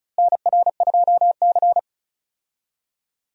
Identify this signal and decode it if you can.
Morse code, heard as a keyed tone, about 36 wpm, 700 Hz. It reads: NR2C
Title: USA Call Signs - 4 Characters - Repeated 3x - 35wpm